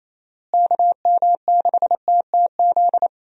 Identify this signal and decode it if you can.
Morse code, heard as a keyed tone, about 28 wpm, 700 Hz. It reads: KM6TTZ
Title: USA Call Signs - 6 Characters - Repeated 2x - 28wpm